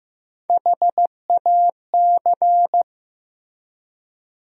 Morse code, heard as a keyed tone, about 15 wpm, 700 Hz. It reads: HAC
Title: USA Call Sign Postfixes - 3 Letters 15wpm